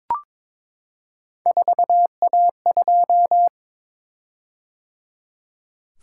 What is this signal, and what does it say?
Morse code, heard as a keyed tone, about 22 wpm, 700 Hz. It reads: 4A2
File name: practice / 22wpm / Three Letter-Number Combinations - 22wpm